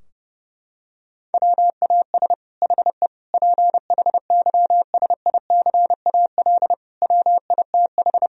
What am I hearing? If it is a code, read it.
Morse code, heard as a keyed tone, about 32 wpm, 700 Hz. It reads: WAS HE PHYSICAL WITH
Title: Sentences from Top 1000 Words - Review Part 4 - 30wpm